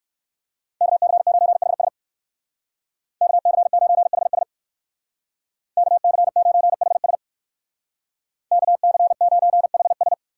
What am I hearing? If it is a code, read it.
Morse code, heard as a keyed tone, about 59 wpm, 700 Hz. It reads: KC9HS KC9HS KC9HS KC9HS